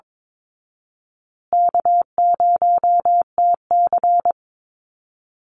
Morse code, heard as a keyed tone, about 22 wpm, 700 Hz. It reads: K0TC
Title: Call Signs Repeated 2x - 22wpm